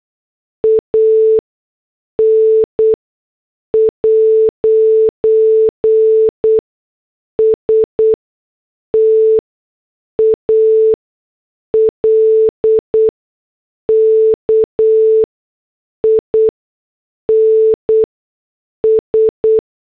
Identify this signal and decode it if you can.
Morse code, heard as a keyed tone, about 8 wpm, spaced as for 6 wpm, 440 Hz. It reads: AN'S TALKIN S